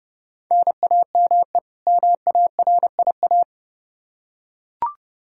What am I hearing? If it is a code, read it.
Morse code, heard as a keyed tone, about 30 wpm, 700 Hz. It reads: NAME MARIA
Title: Practice QSO - Exchange 1 - Beginning - 30wpm